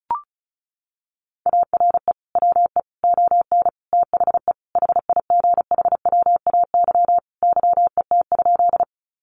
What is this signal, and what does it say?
Morse code, heard as a keyed tone, about 34 wpm, 700 Hz. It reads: ARE WE ON THE HIGHWAY YET?
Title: Sentences from Top 1400 Words 35wpm